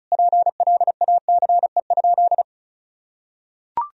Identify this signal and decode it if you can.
Morse code, heard as a keyed tone, about 35 wpm, 700 Hz. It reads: PLACE?
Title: Sentences from Top 200 Words and Abbreviations 35wpm